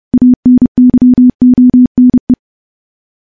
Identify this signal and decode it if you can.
Morse code, heard as a keyed tone, about 28 wpm, 250 Hz. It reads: ANYONE
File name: morse / morse-training-30WPM